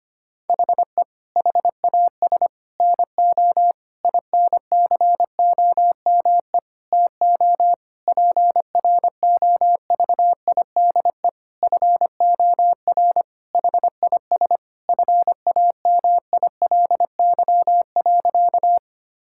Morse code, heard as a keyed tone, about 25 wpm, 700 Hz. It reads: HE HAS NO INCOME TO PROVIDE FOR HIS FAMILY.
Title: Sentences from Top 1000 Words - Review Part 5 - 25wpm